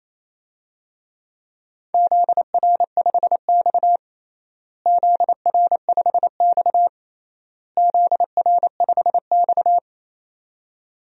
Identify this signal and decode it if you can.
Morse code, heard as a keyed tone, about 28 wpm, 700 Hz. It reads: ZR5X ZR5X ZR5X
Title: DXCC-100 Call Signs - 4 Characters - Repeated 3x - 28wpm